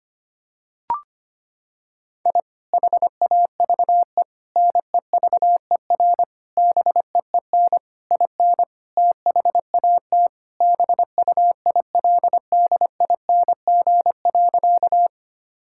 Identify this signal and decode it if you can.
Morse code, heard as a keyed tone, about 25 wpm, 700 Hz. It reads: I HAVE NEVER BEEN IN THAT BUILDING.
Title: Sentences from Top 1000 Words - Review Part 5 - 25wpm